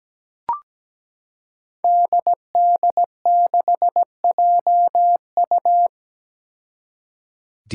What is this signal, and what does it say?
Morse code, heard as a keyed tone, about 17 wpm, 700 Hz. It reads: DD6JU
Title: Five Challenging Letter-Number Combinations 17wpm